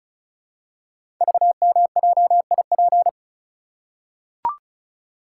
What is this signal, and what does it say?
Morse code, heard as a keyed tone, about 36 wpm, 700 Hz. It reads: VMJIP
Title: Code Groups - 5 Letter:Numbers - 35wpm